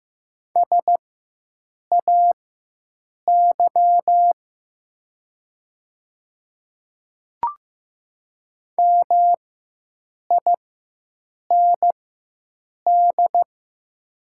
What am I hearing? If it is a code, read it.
Morse code, heard as a keyed tone, about 15 wpm, spaced as for 7 wpm, 700 Hz. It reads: SAY MIND
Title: 4x Character Spacing - Top 500 Words - 15wpm